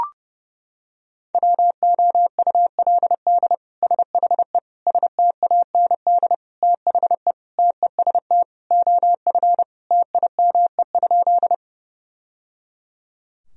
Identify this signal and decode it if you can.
Morse code, heard as a keyed tone, about 30 wpm, 700 Hz. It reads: WOULD SHE STAND THE TEST OF TIME?
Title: Sentences from Top 1000 Words - Review Part 3 - 30wpm